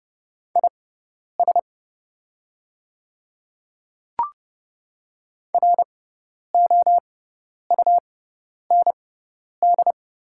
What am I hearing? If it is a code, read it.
Morse code, heard as a keyed tone, about 30 wpm, spaced as for 10 wpm, 700 Hz. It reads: IS ROUND
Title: 6x Character Spacing - Top 200 Words - 30wpm